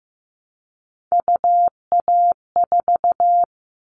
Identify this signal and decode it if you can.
Morse code, heard as a keyed tone, about 15 wpm, 700 Hz. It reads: UA4
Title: Callsign Prefixes for Top 25 DXCC - 15wpm